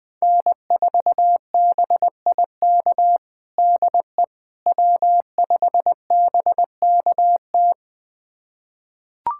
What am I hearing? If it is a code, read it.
Morse code, heard as a keyed tone, about 20 wpm, 700 Hz. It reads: N4BIK DE W5BKT
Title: USA Callsign DE Callsign - 5 Characters - 20wpm